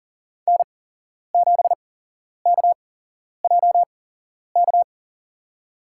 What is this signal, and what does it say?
Morse code, heard as a keyed tone, about 40 wpm, spaced as for 11 wpm, 700 Hz. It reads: N7KJK
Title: USA Call Signs - 5 Characters - 8x Character Spacing - 40wpm